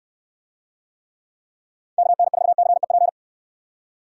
Morse code, heard as a keyed tone, about 69 wpm, 700 Hz. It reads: KN4CEP